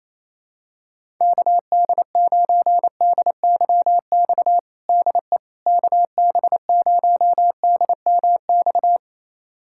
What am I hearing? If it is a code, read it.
Morse code, heard as a keyed tone, about 28 wpm, 700 Hz. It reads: KD9DYX DE KB0DMX
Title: USA Callsign DE Callsign 28wpm